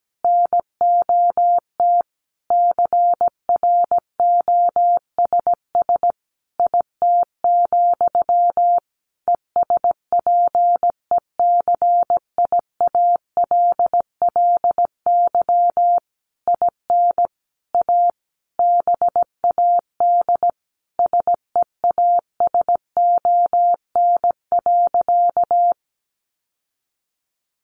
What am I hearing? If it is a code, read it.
Morse code, heard as a keyed tone, about 17 wpm, 700 Hz. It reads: NOT CROSS IT, ESPECIALLY IN A BAD SEASON.
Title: Sentences from Top 1400 Words 17wpm